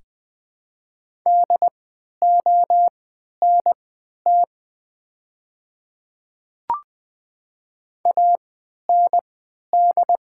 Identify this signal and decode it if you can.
Morse code, heard as a keyed tone, about 20 wpm, spaced as for 11 wpm, 690 Hz. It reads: DONT AND
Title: 3x Character Spacing - Top 300 Words - 20wpm